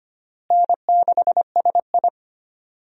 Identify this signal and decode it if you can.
Morse code, heard as a keyed tone, about 25 wpm, 700 Hz. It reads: N6SI
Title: USA Call Signs - 4 Characters - Repeated 3x - 25wpm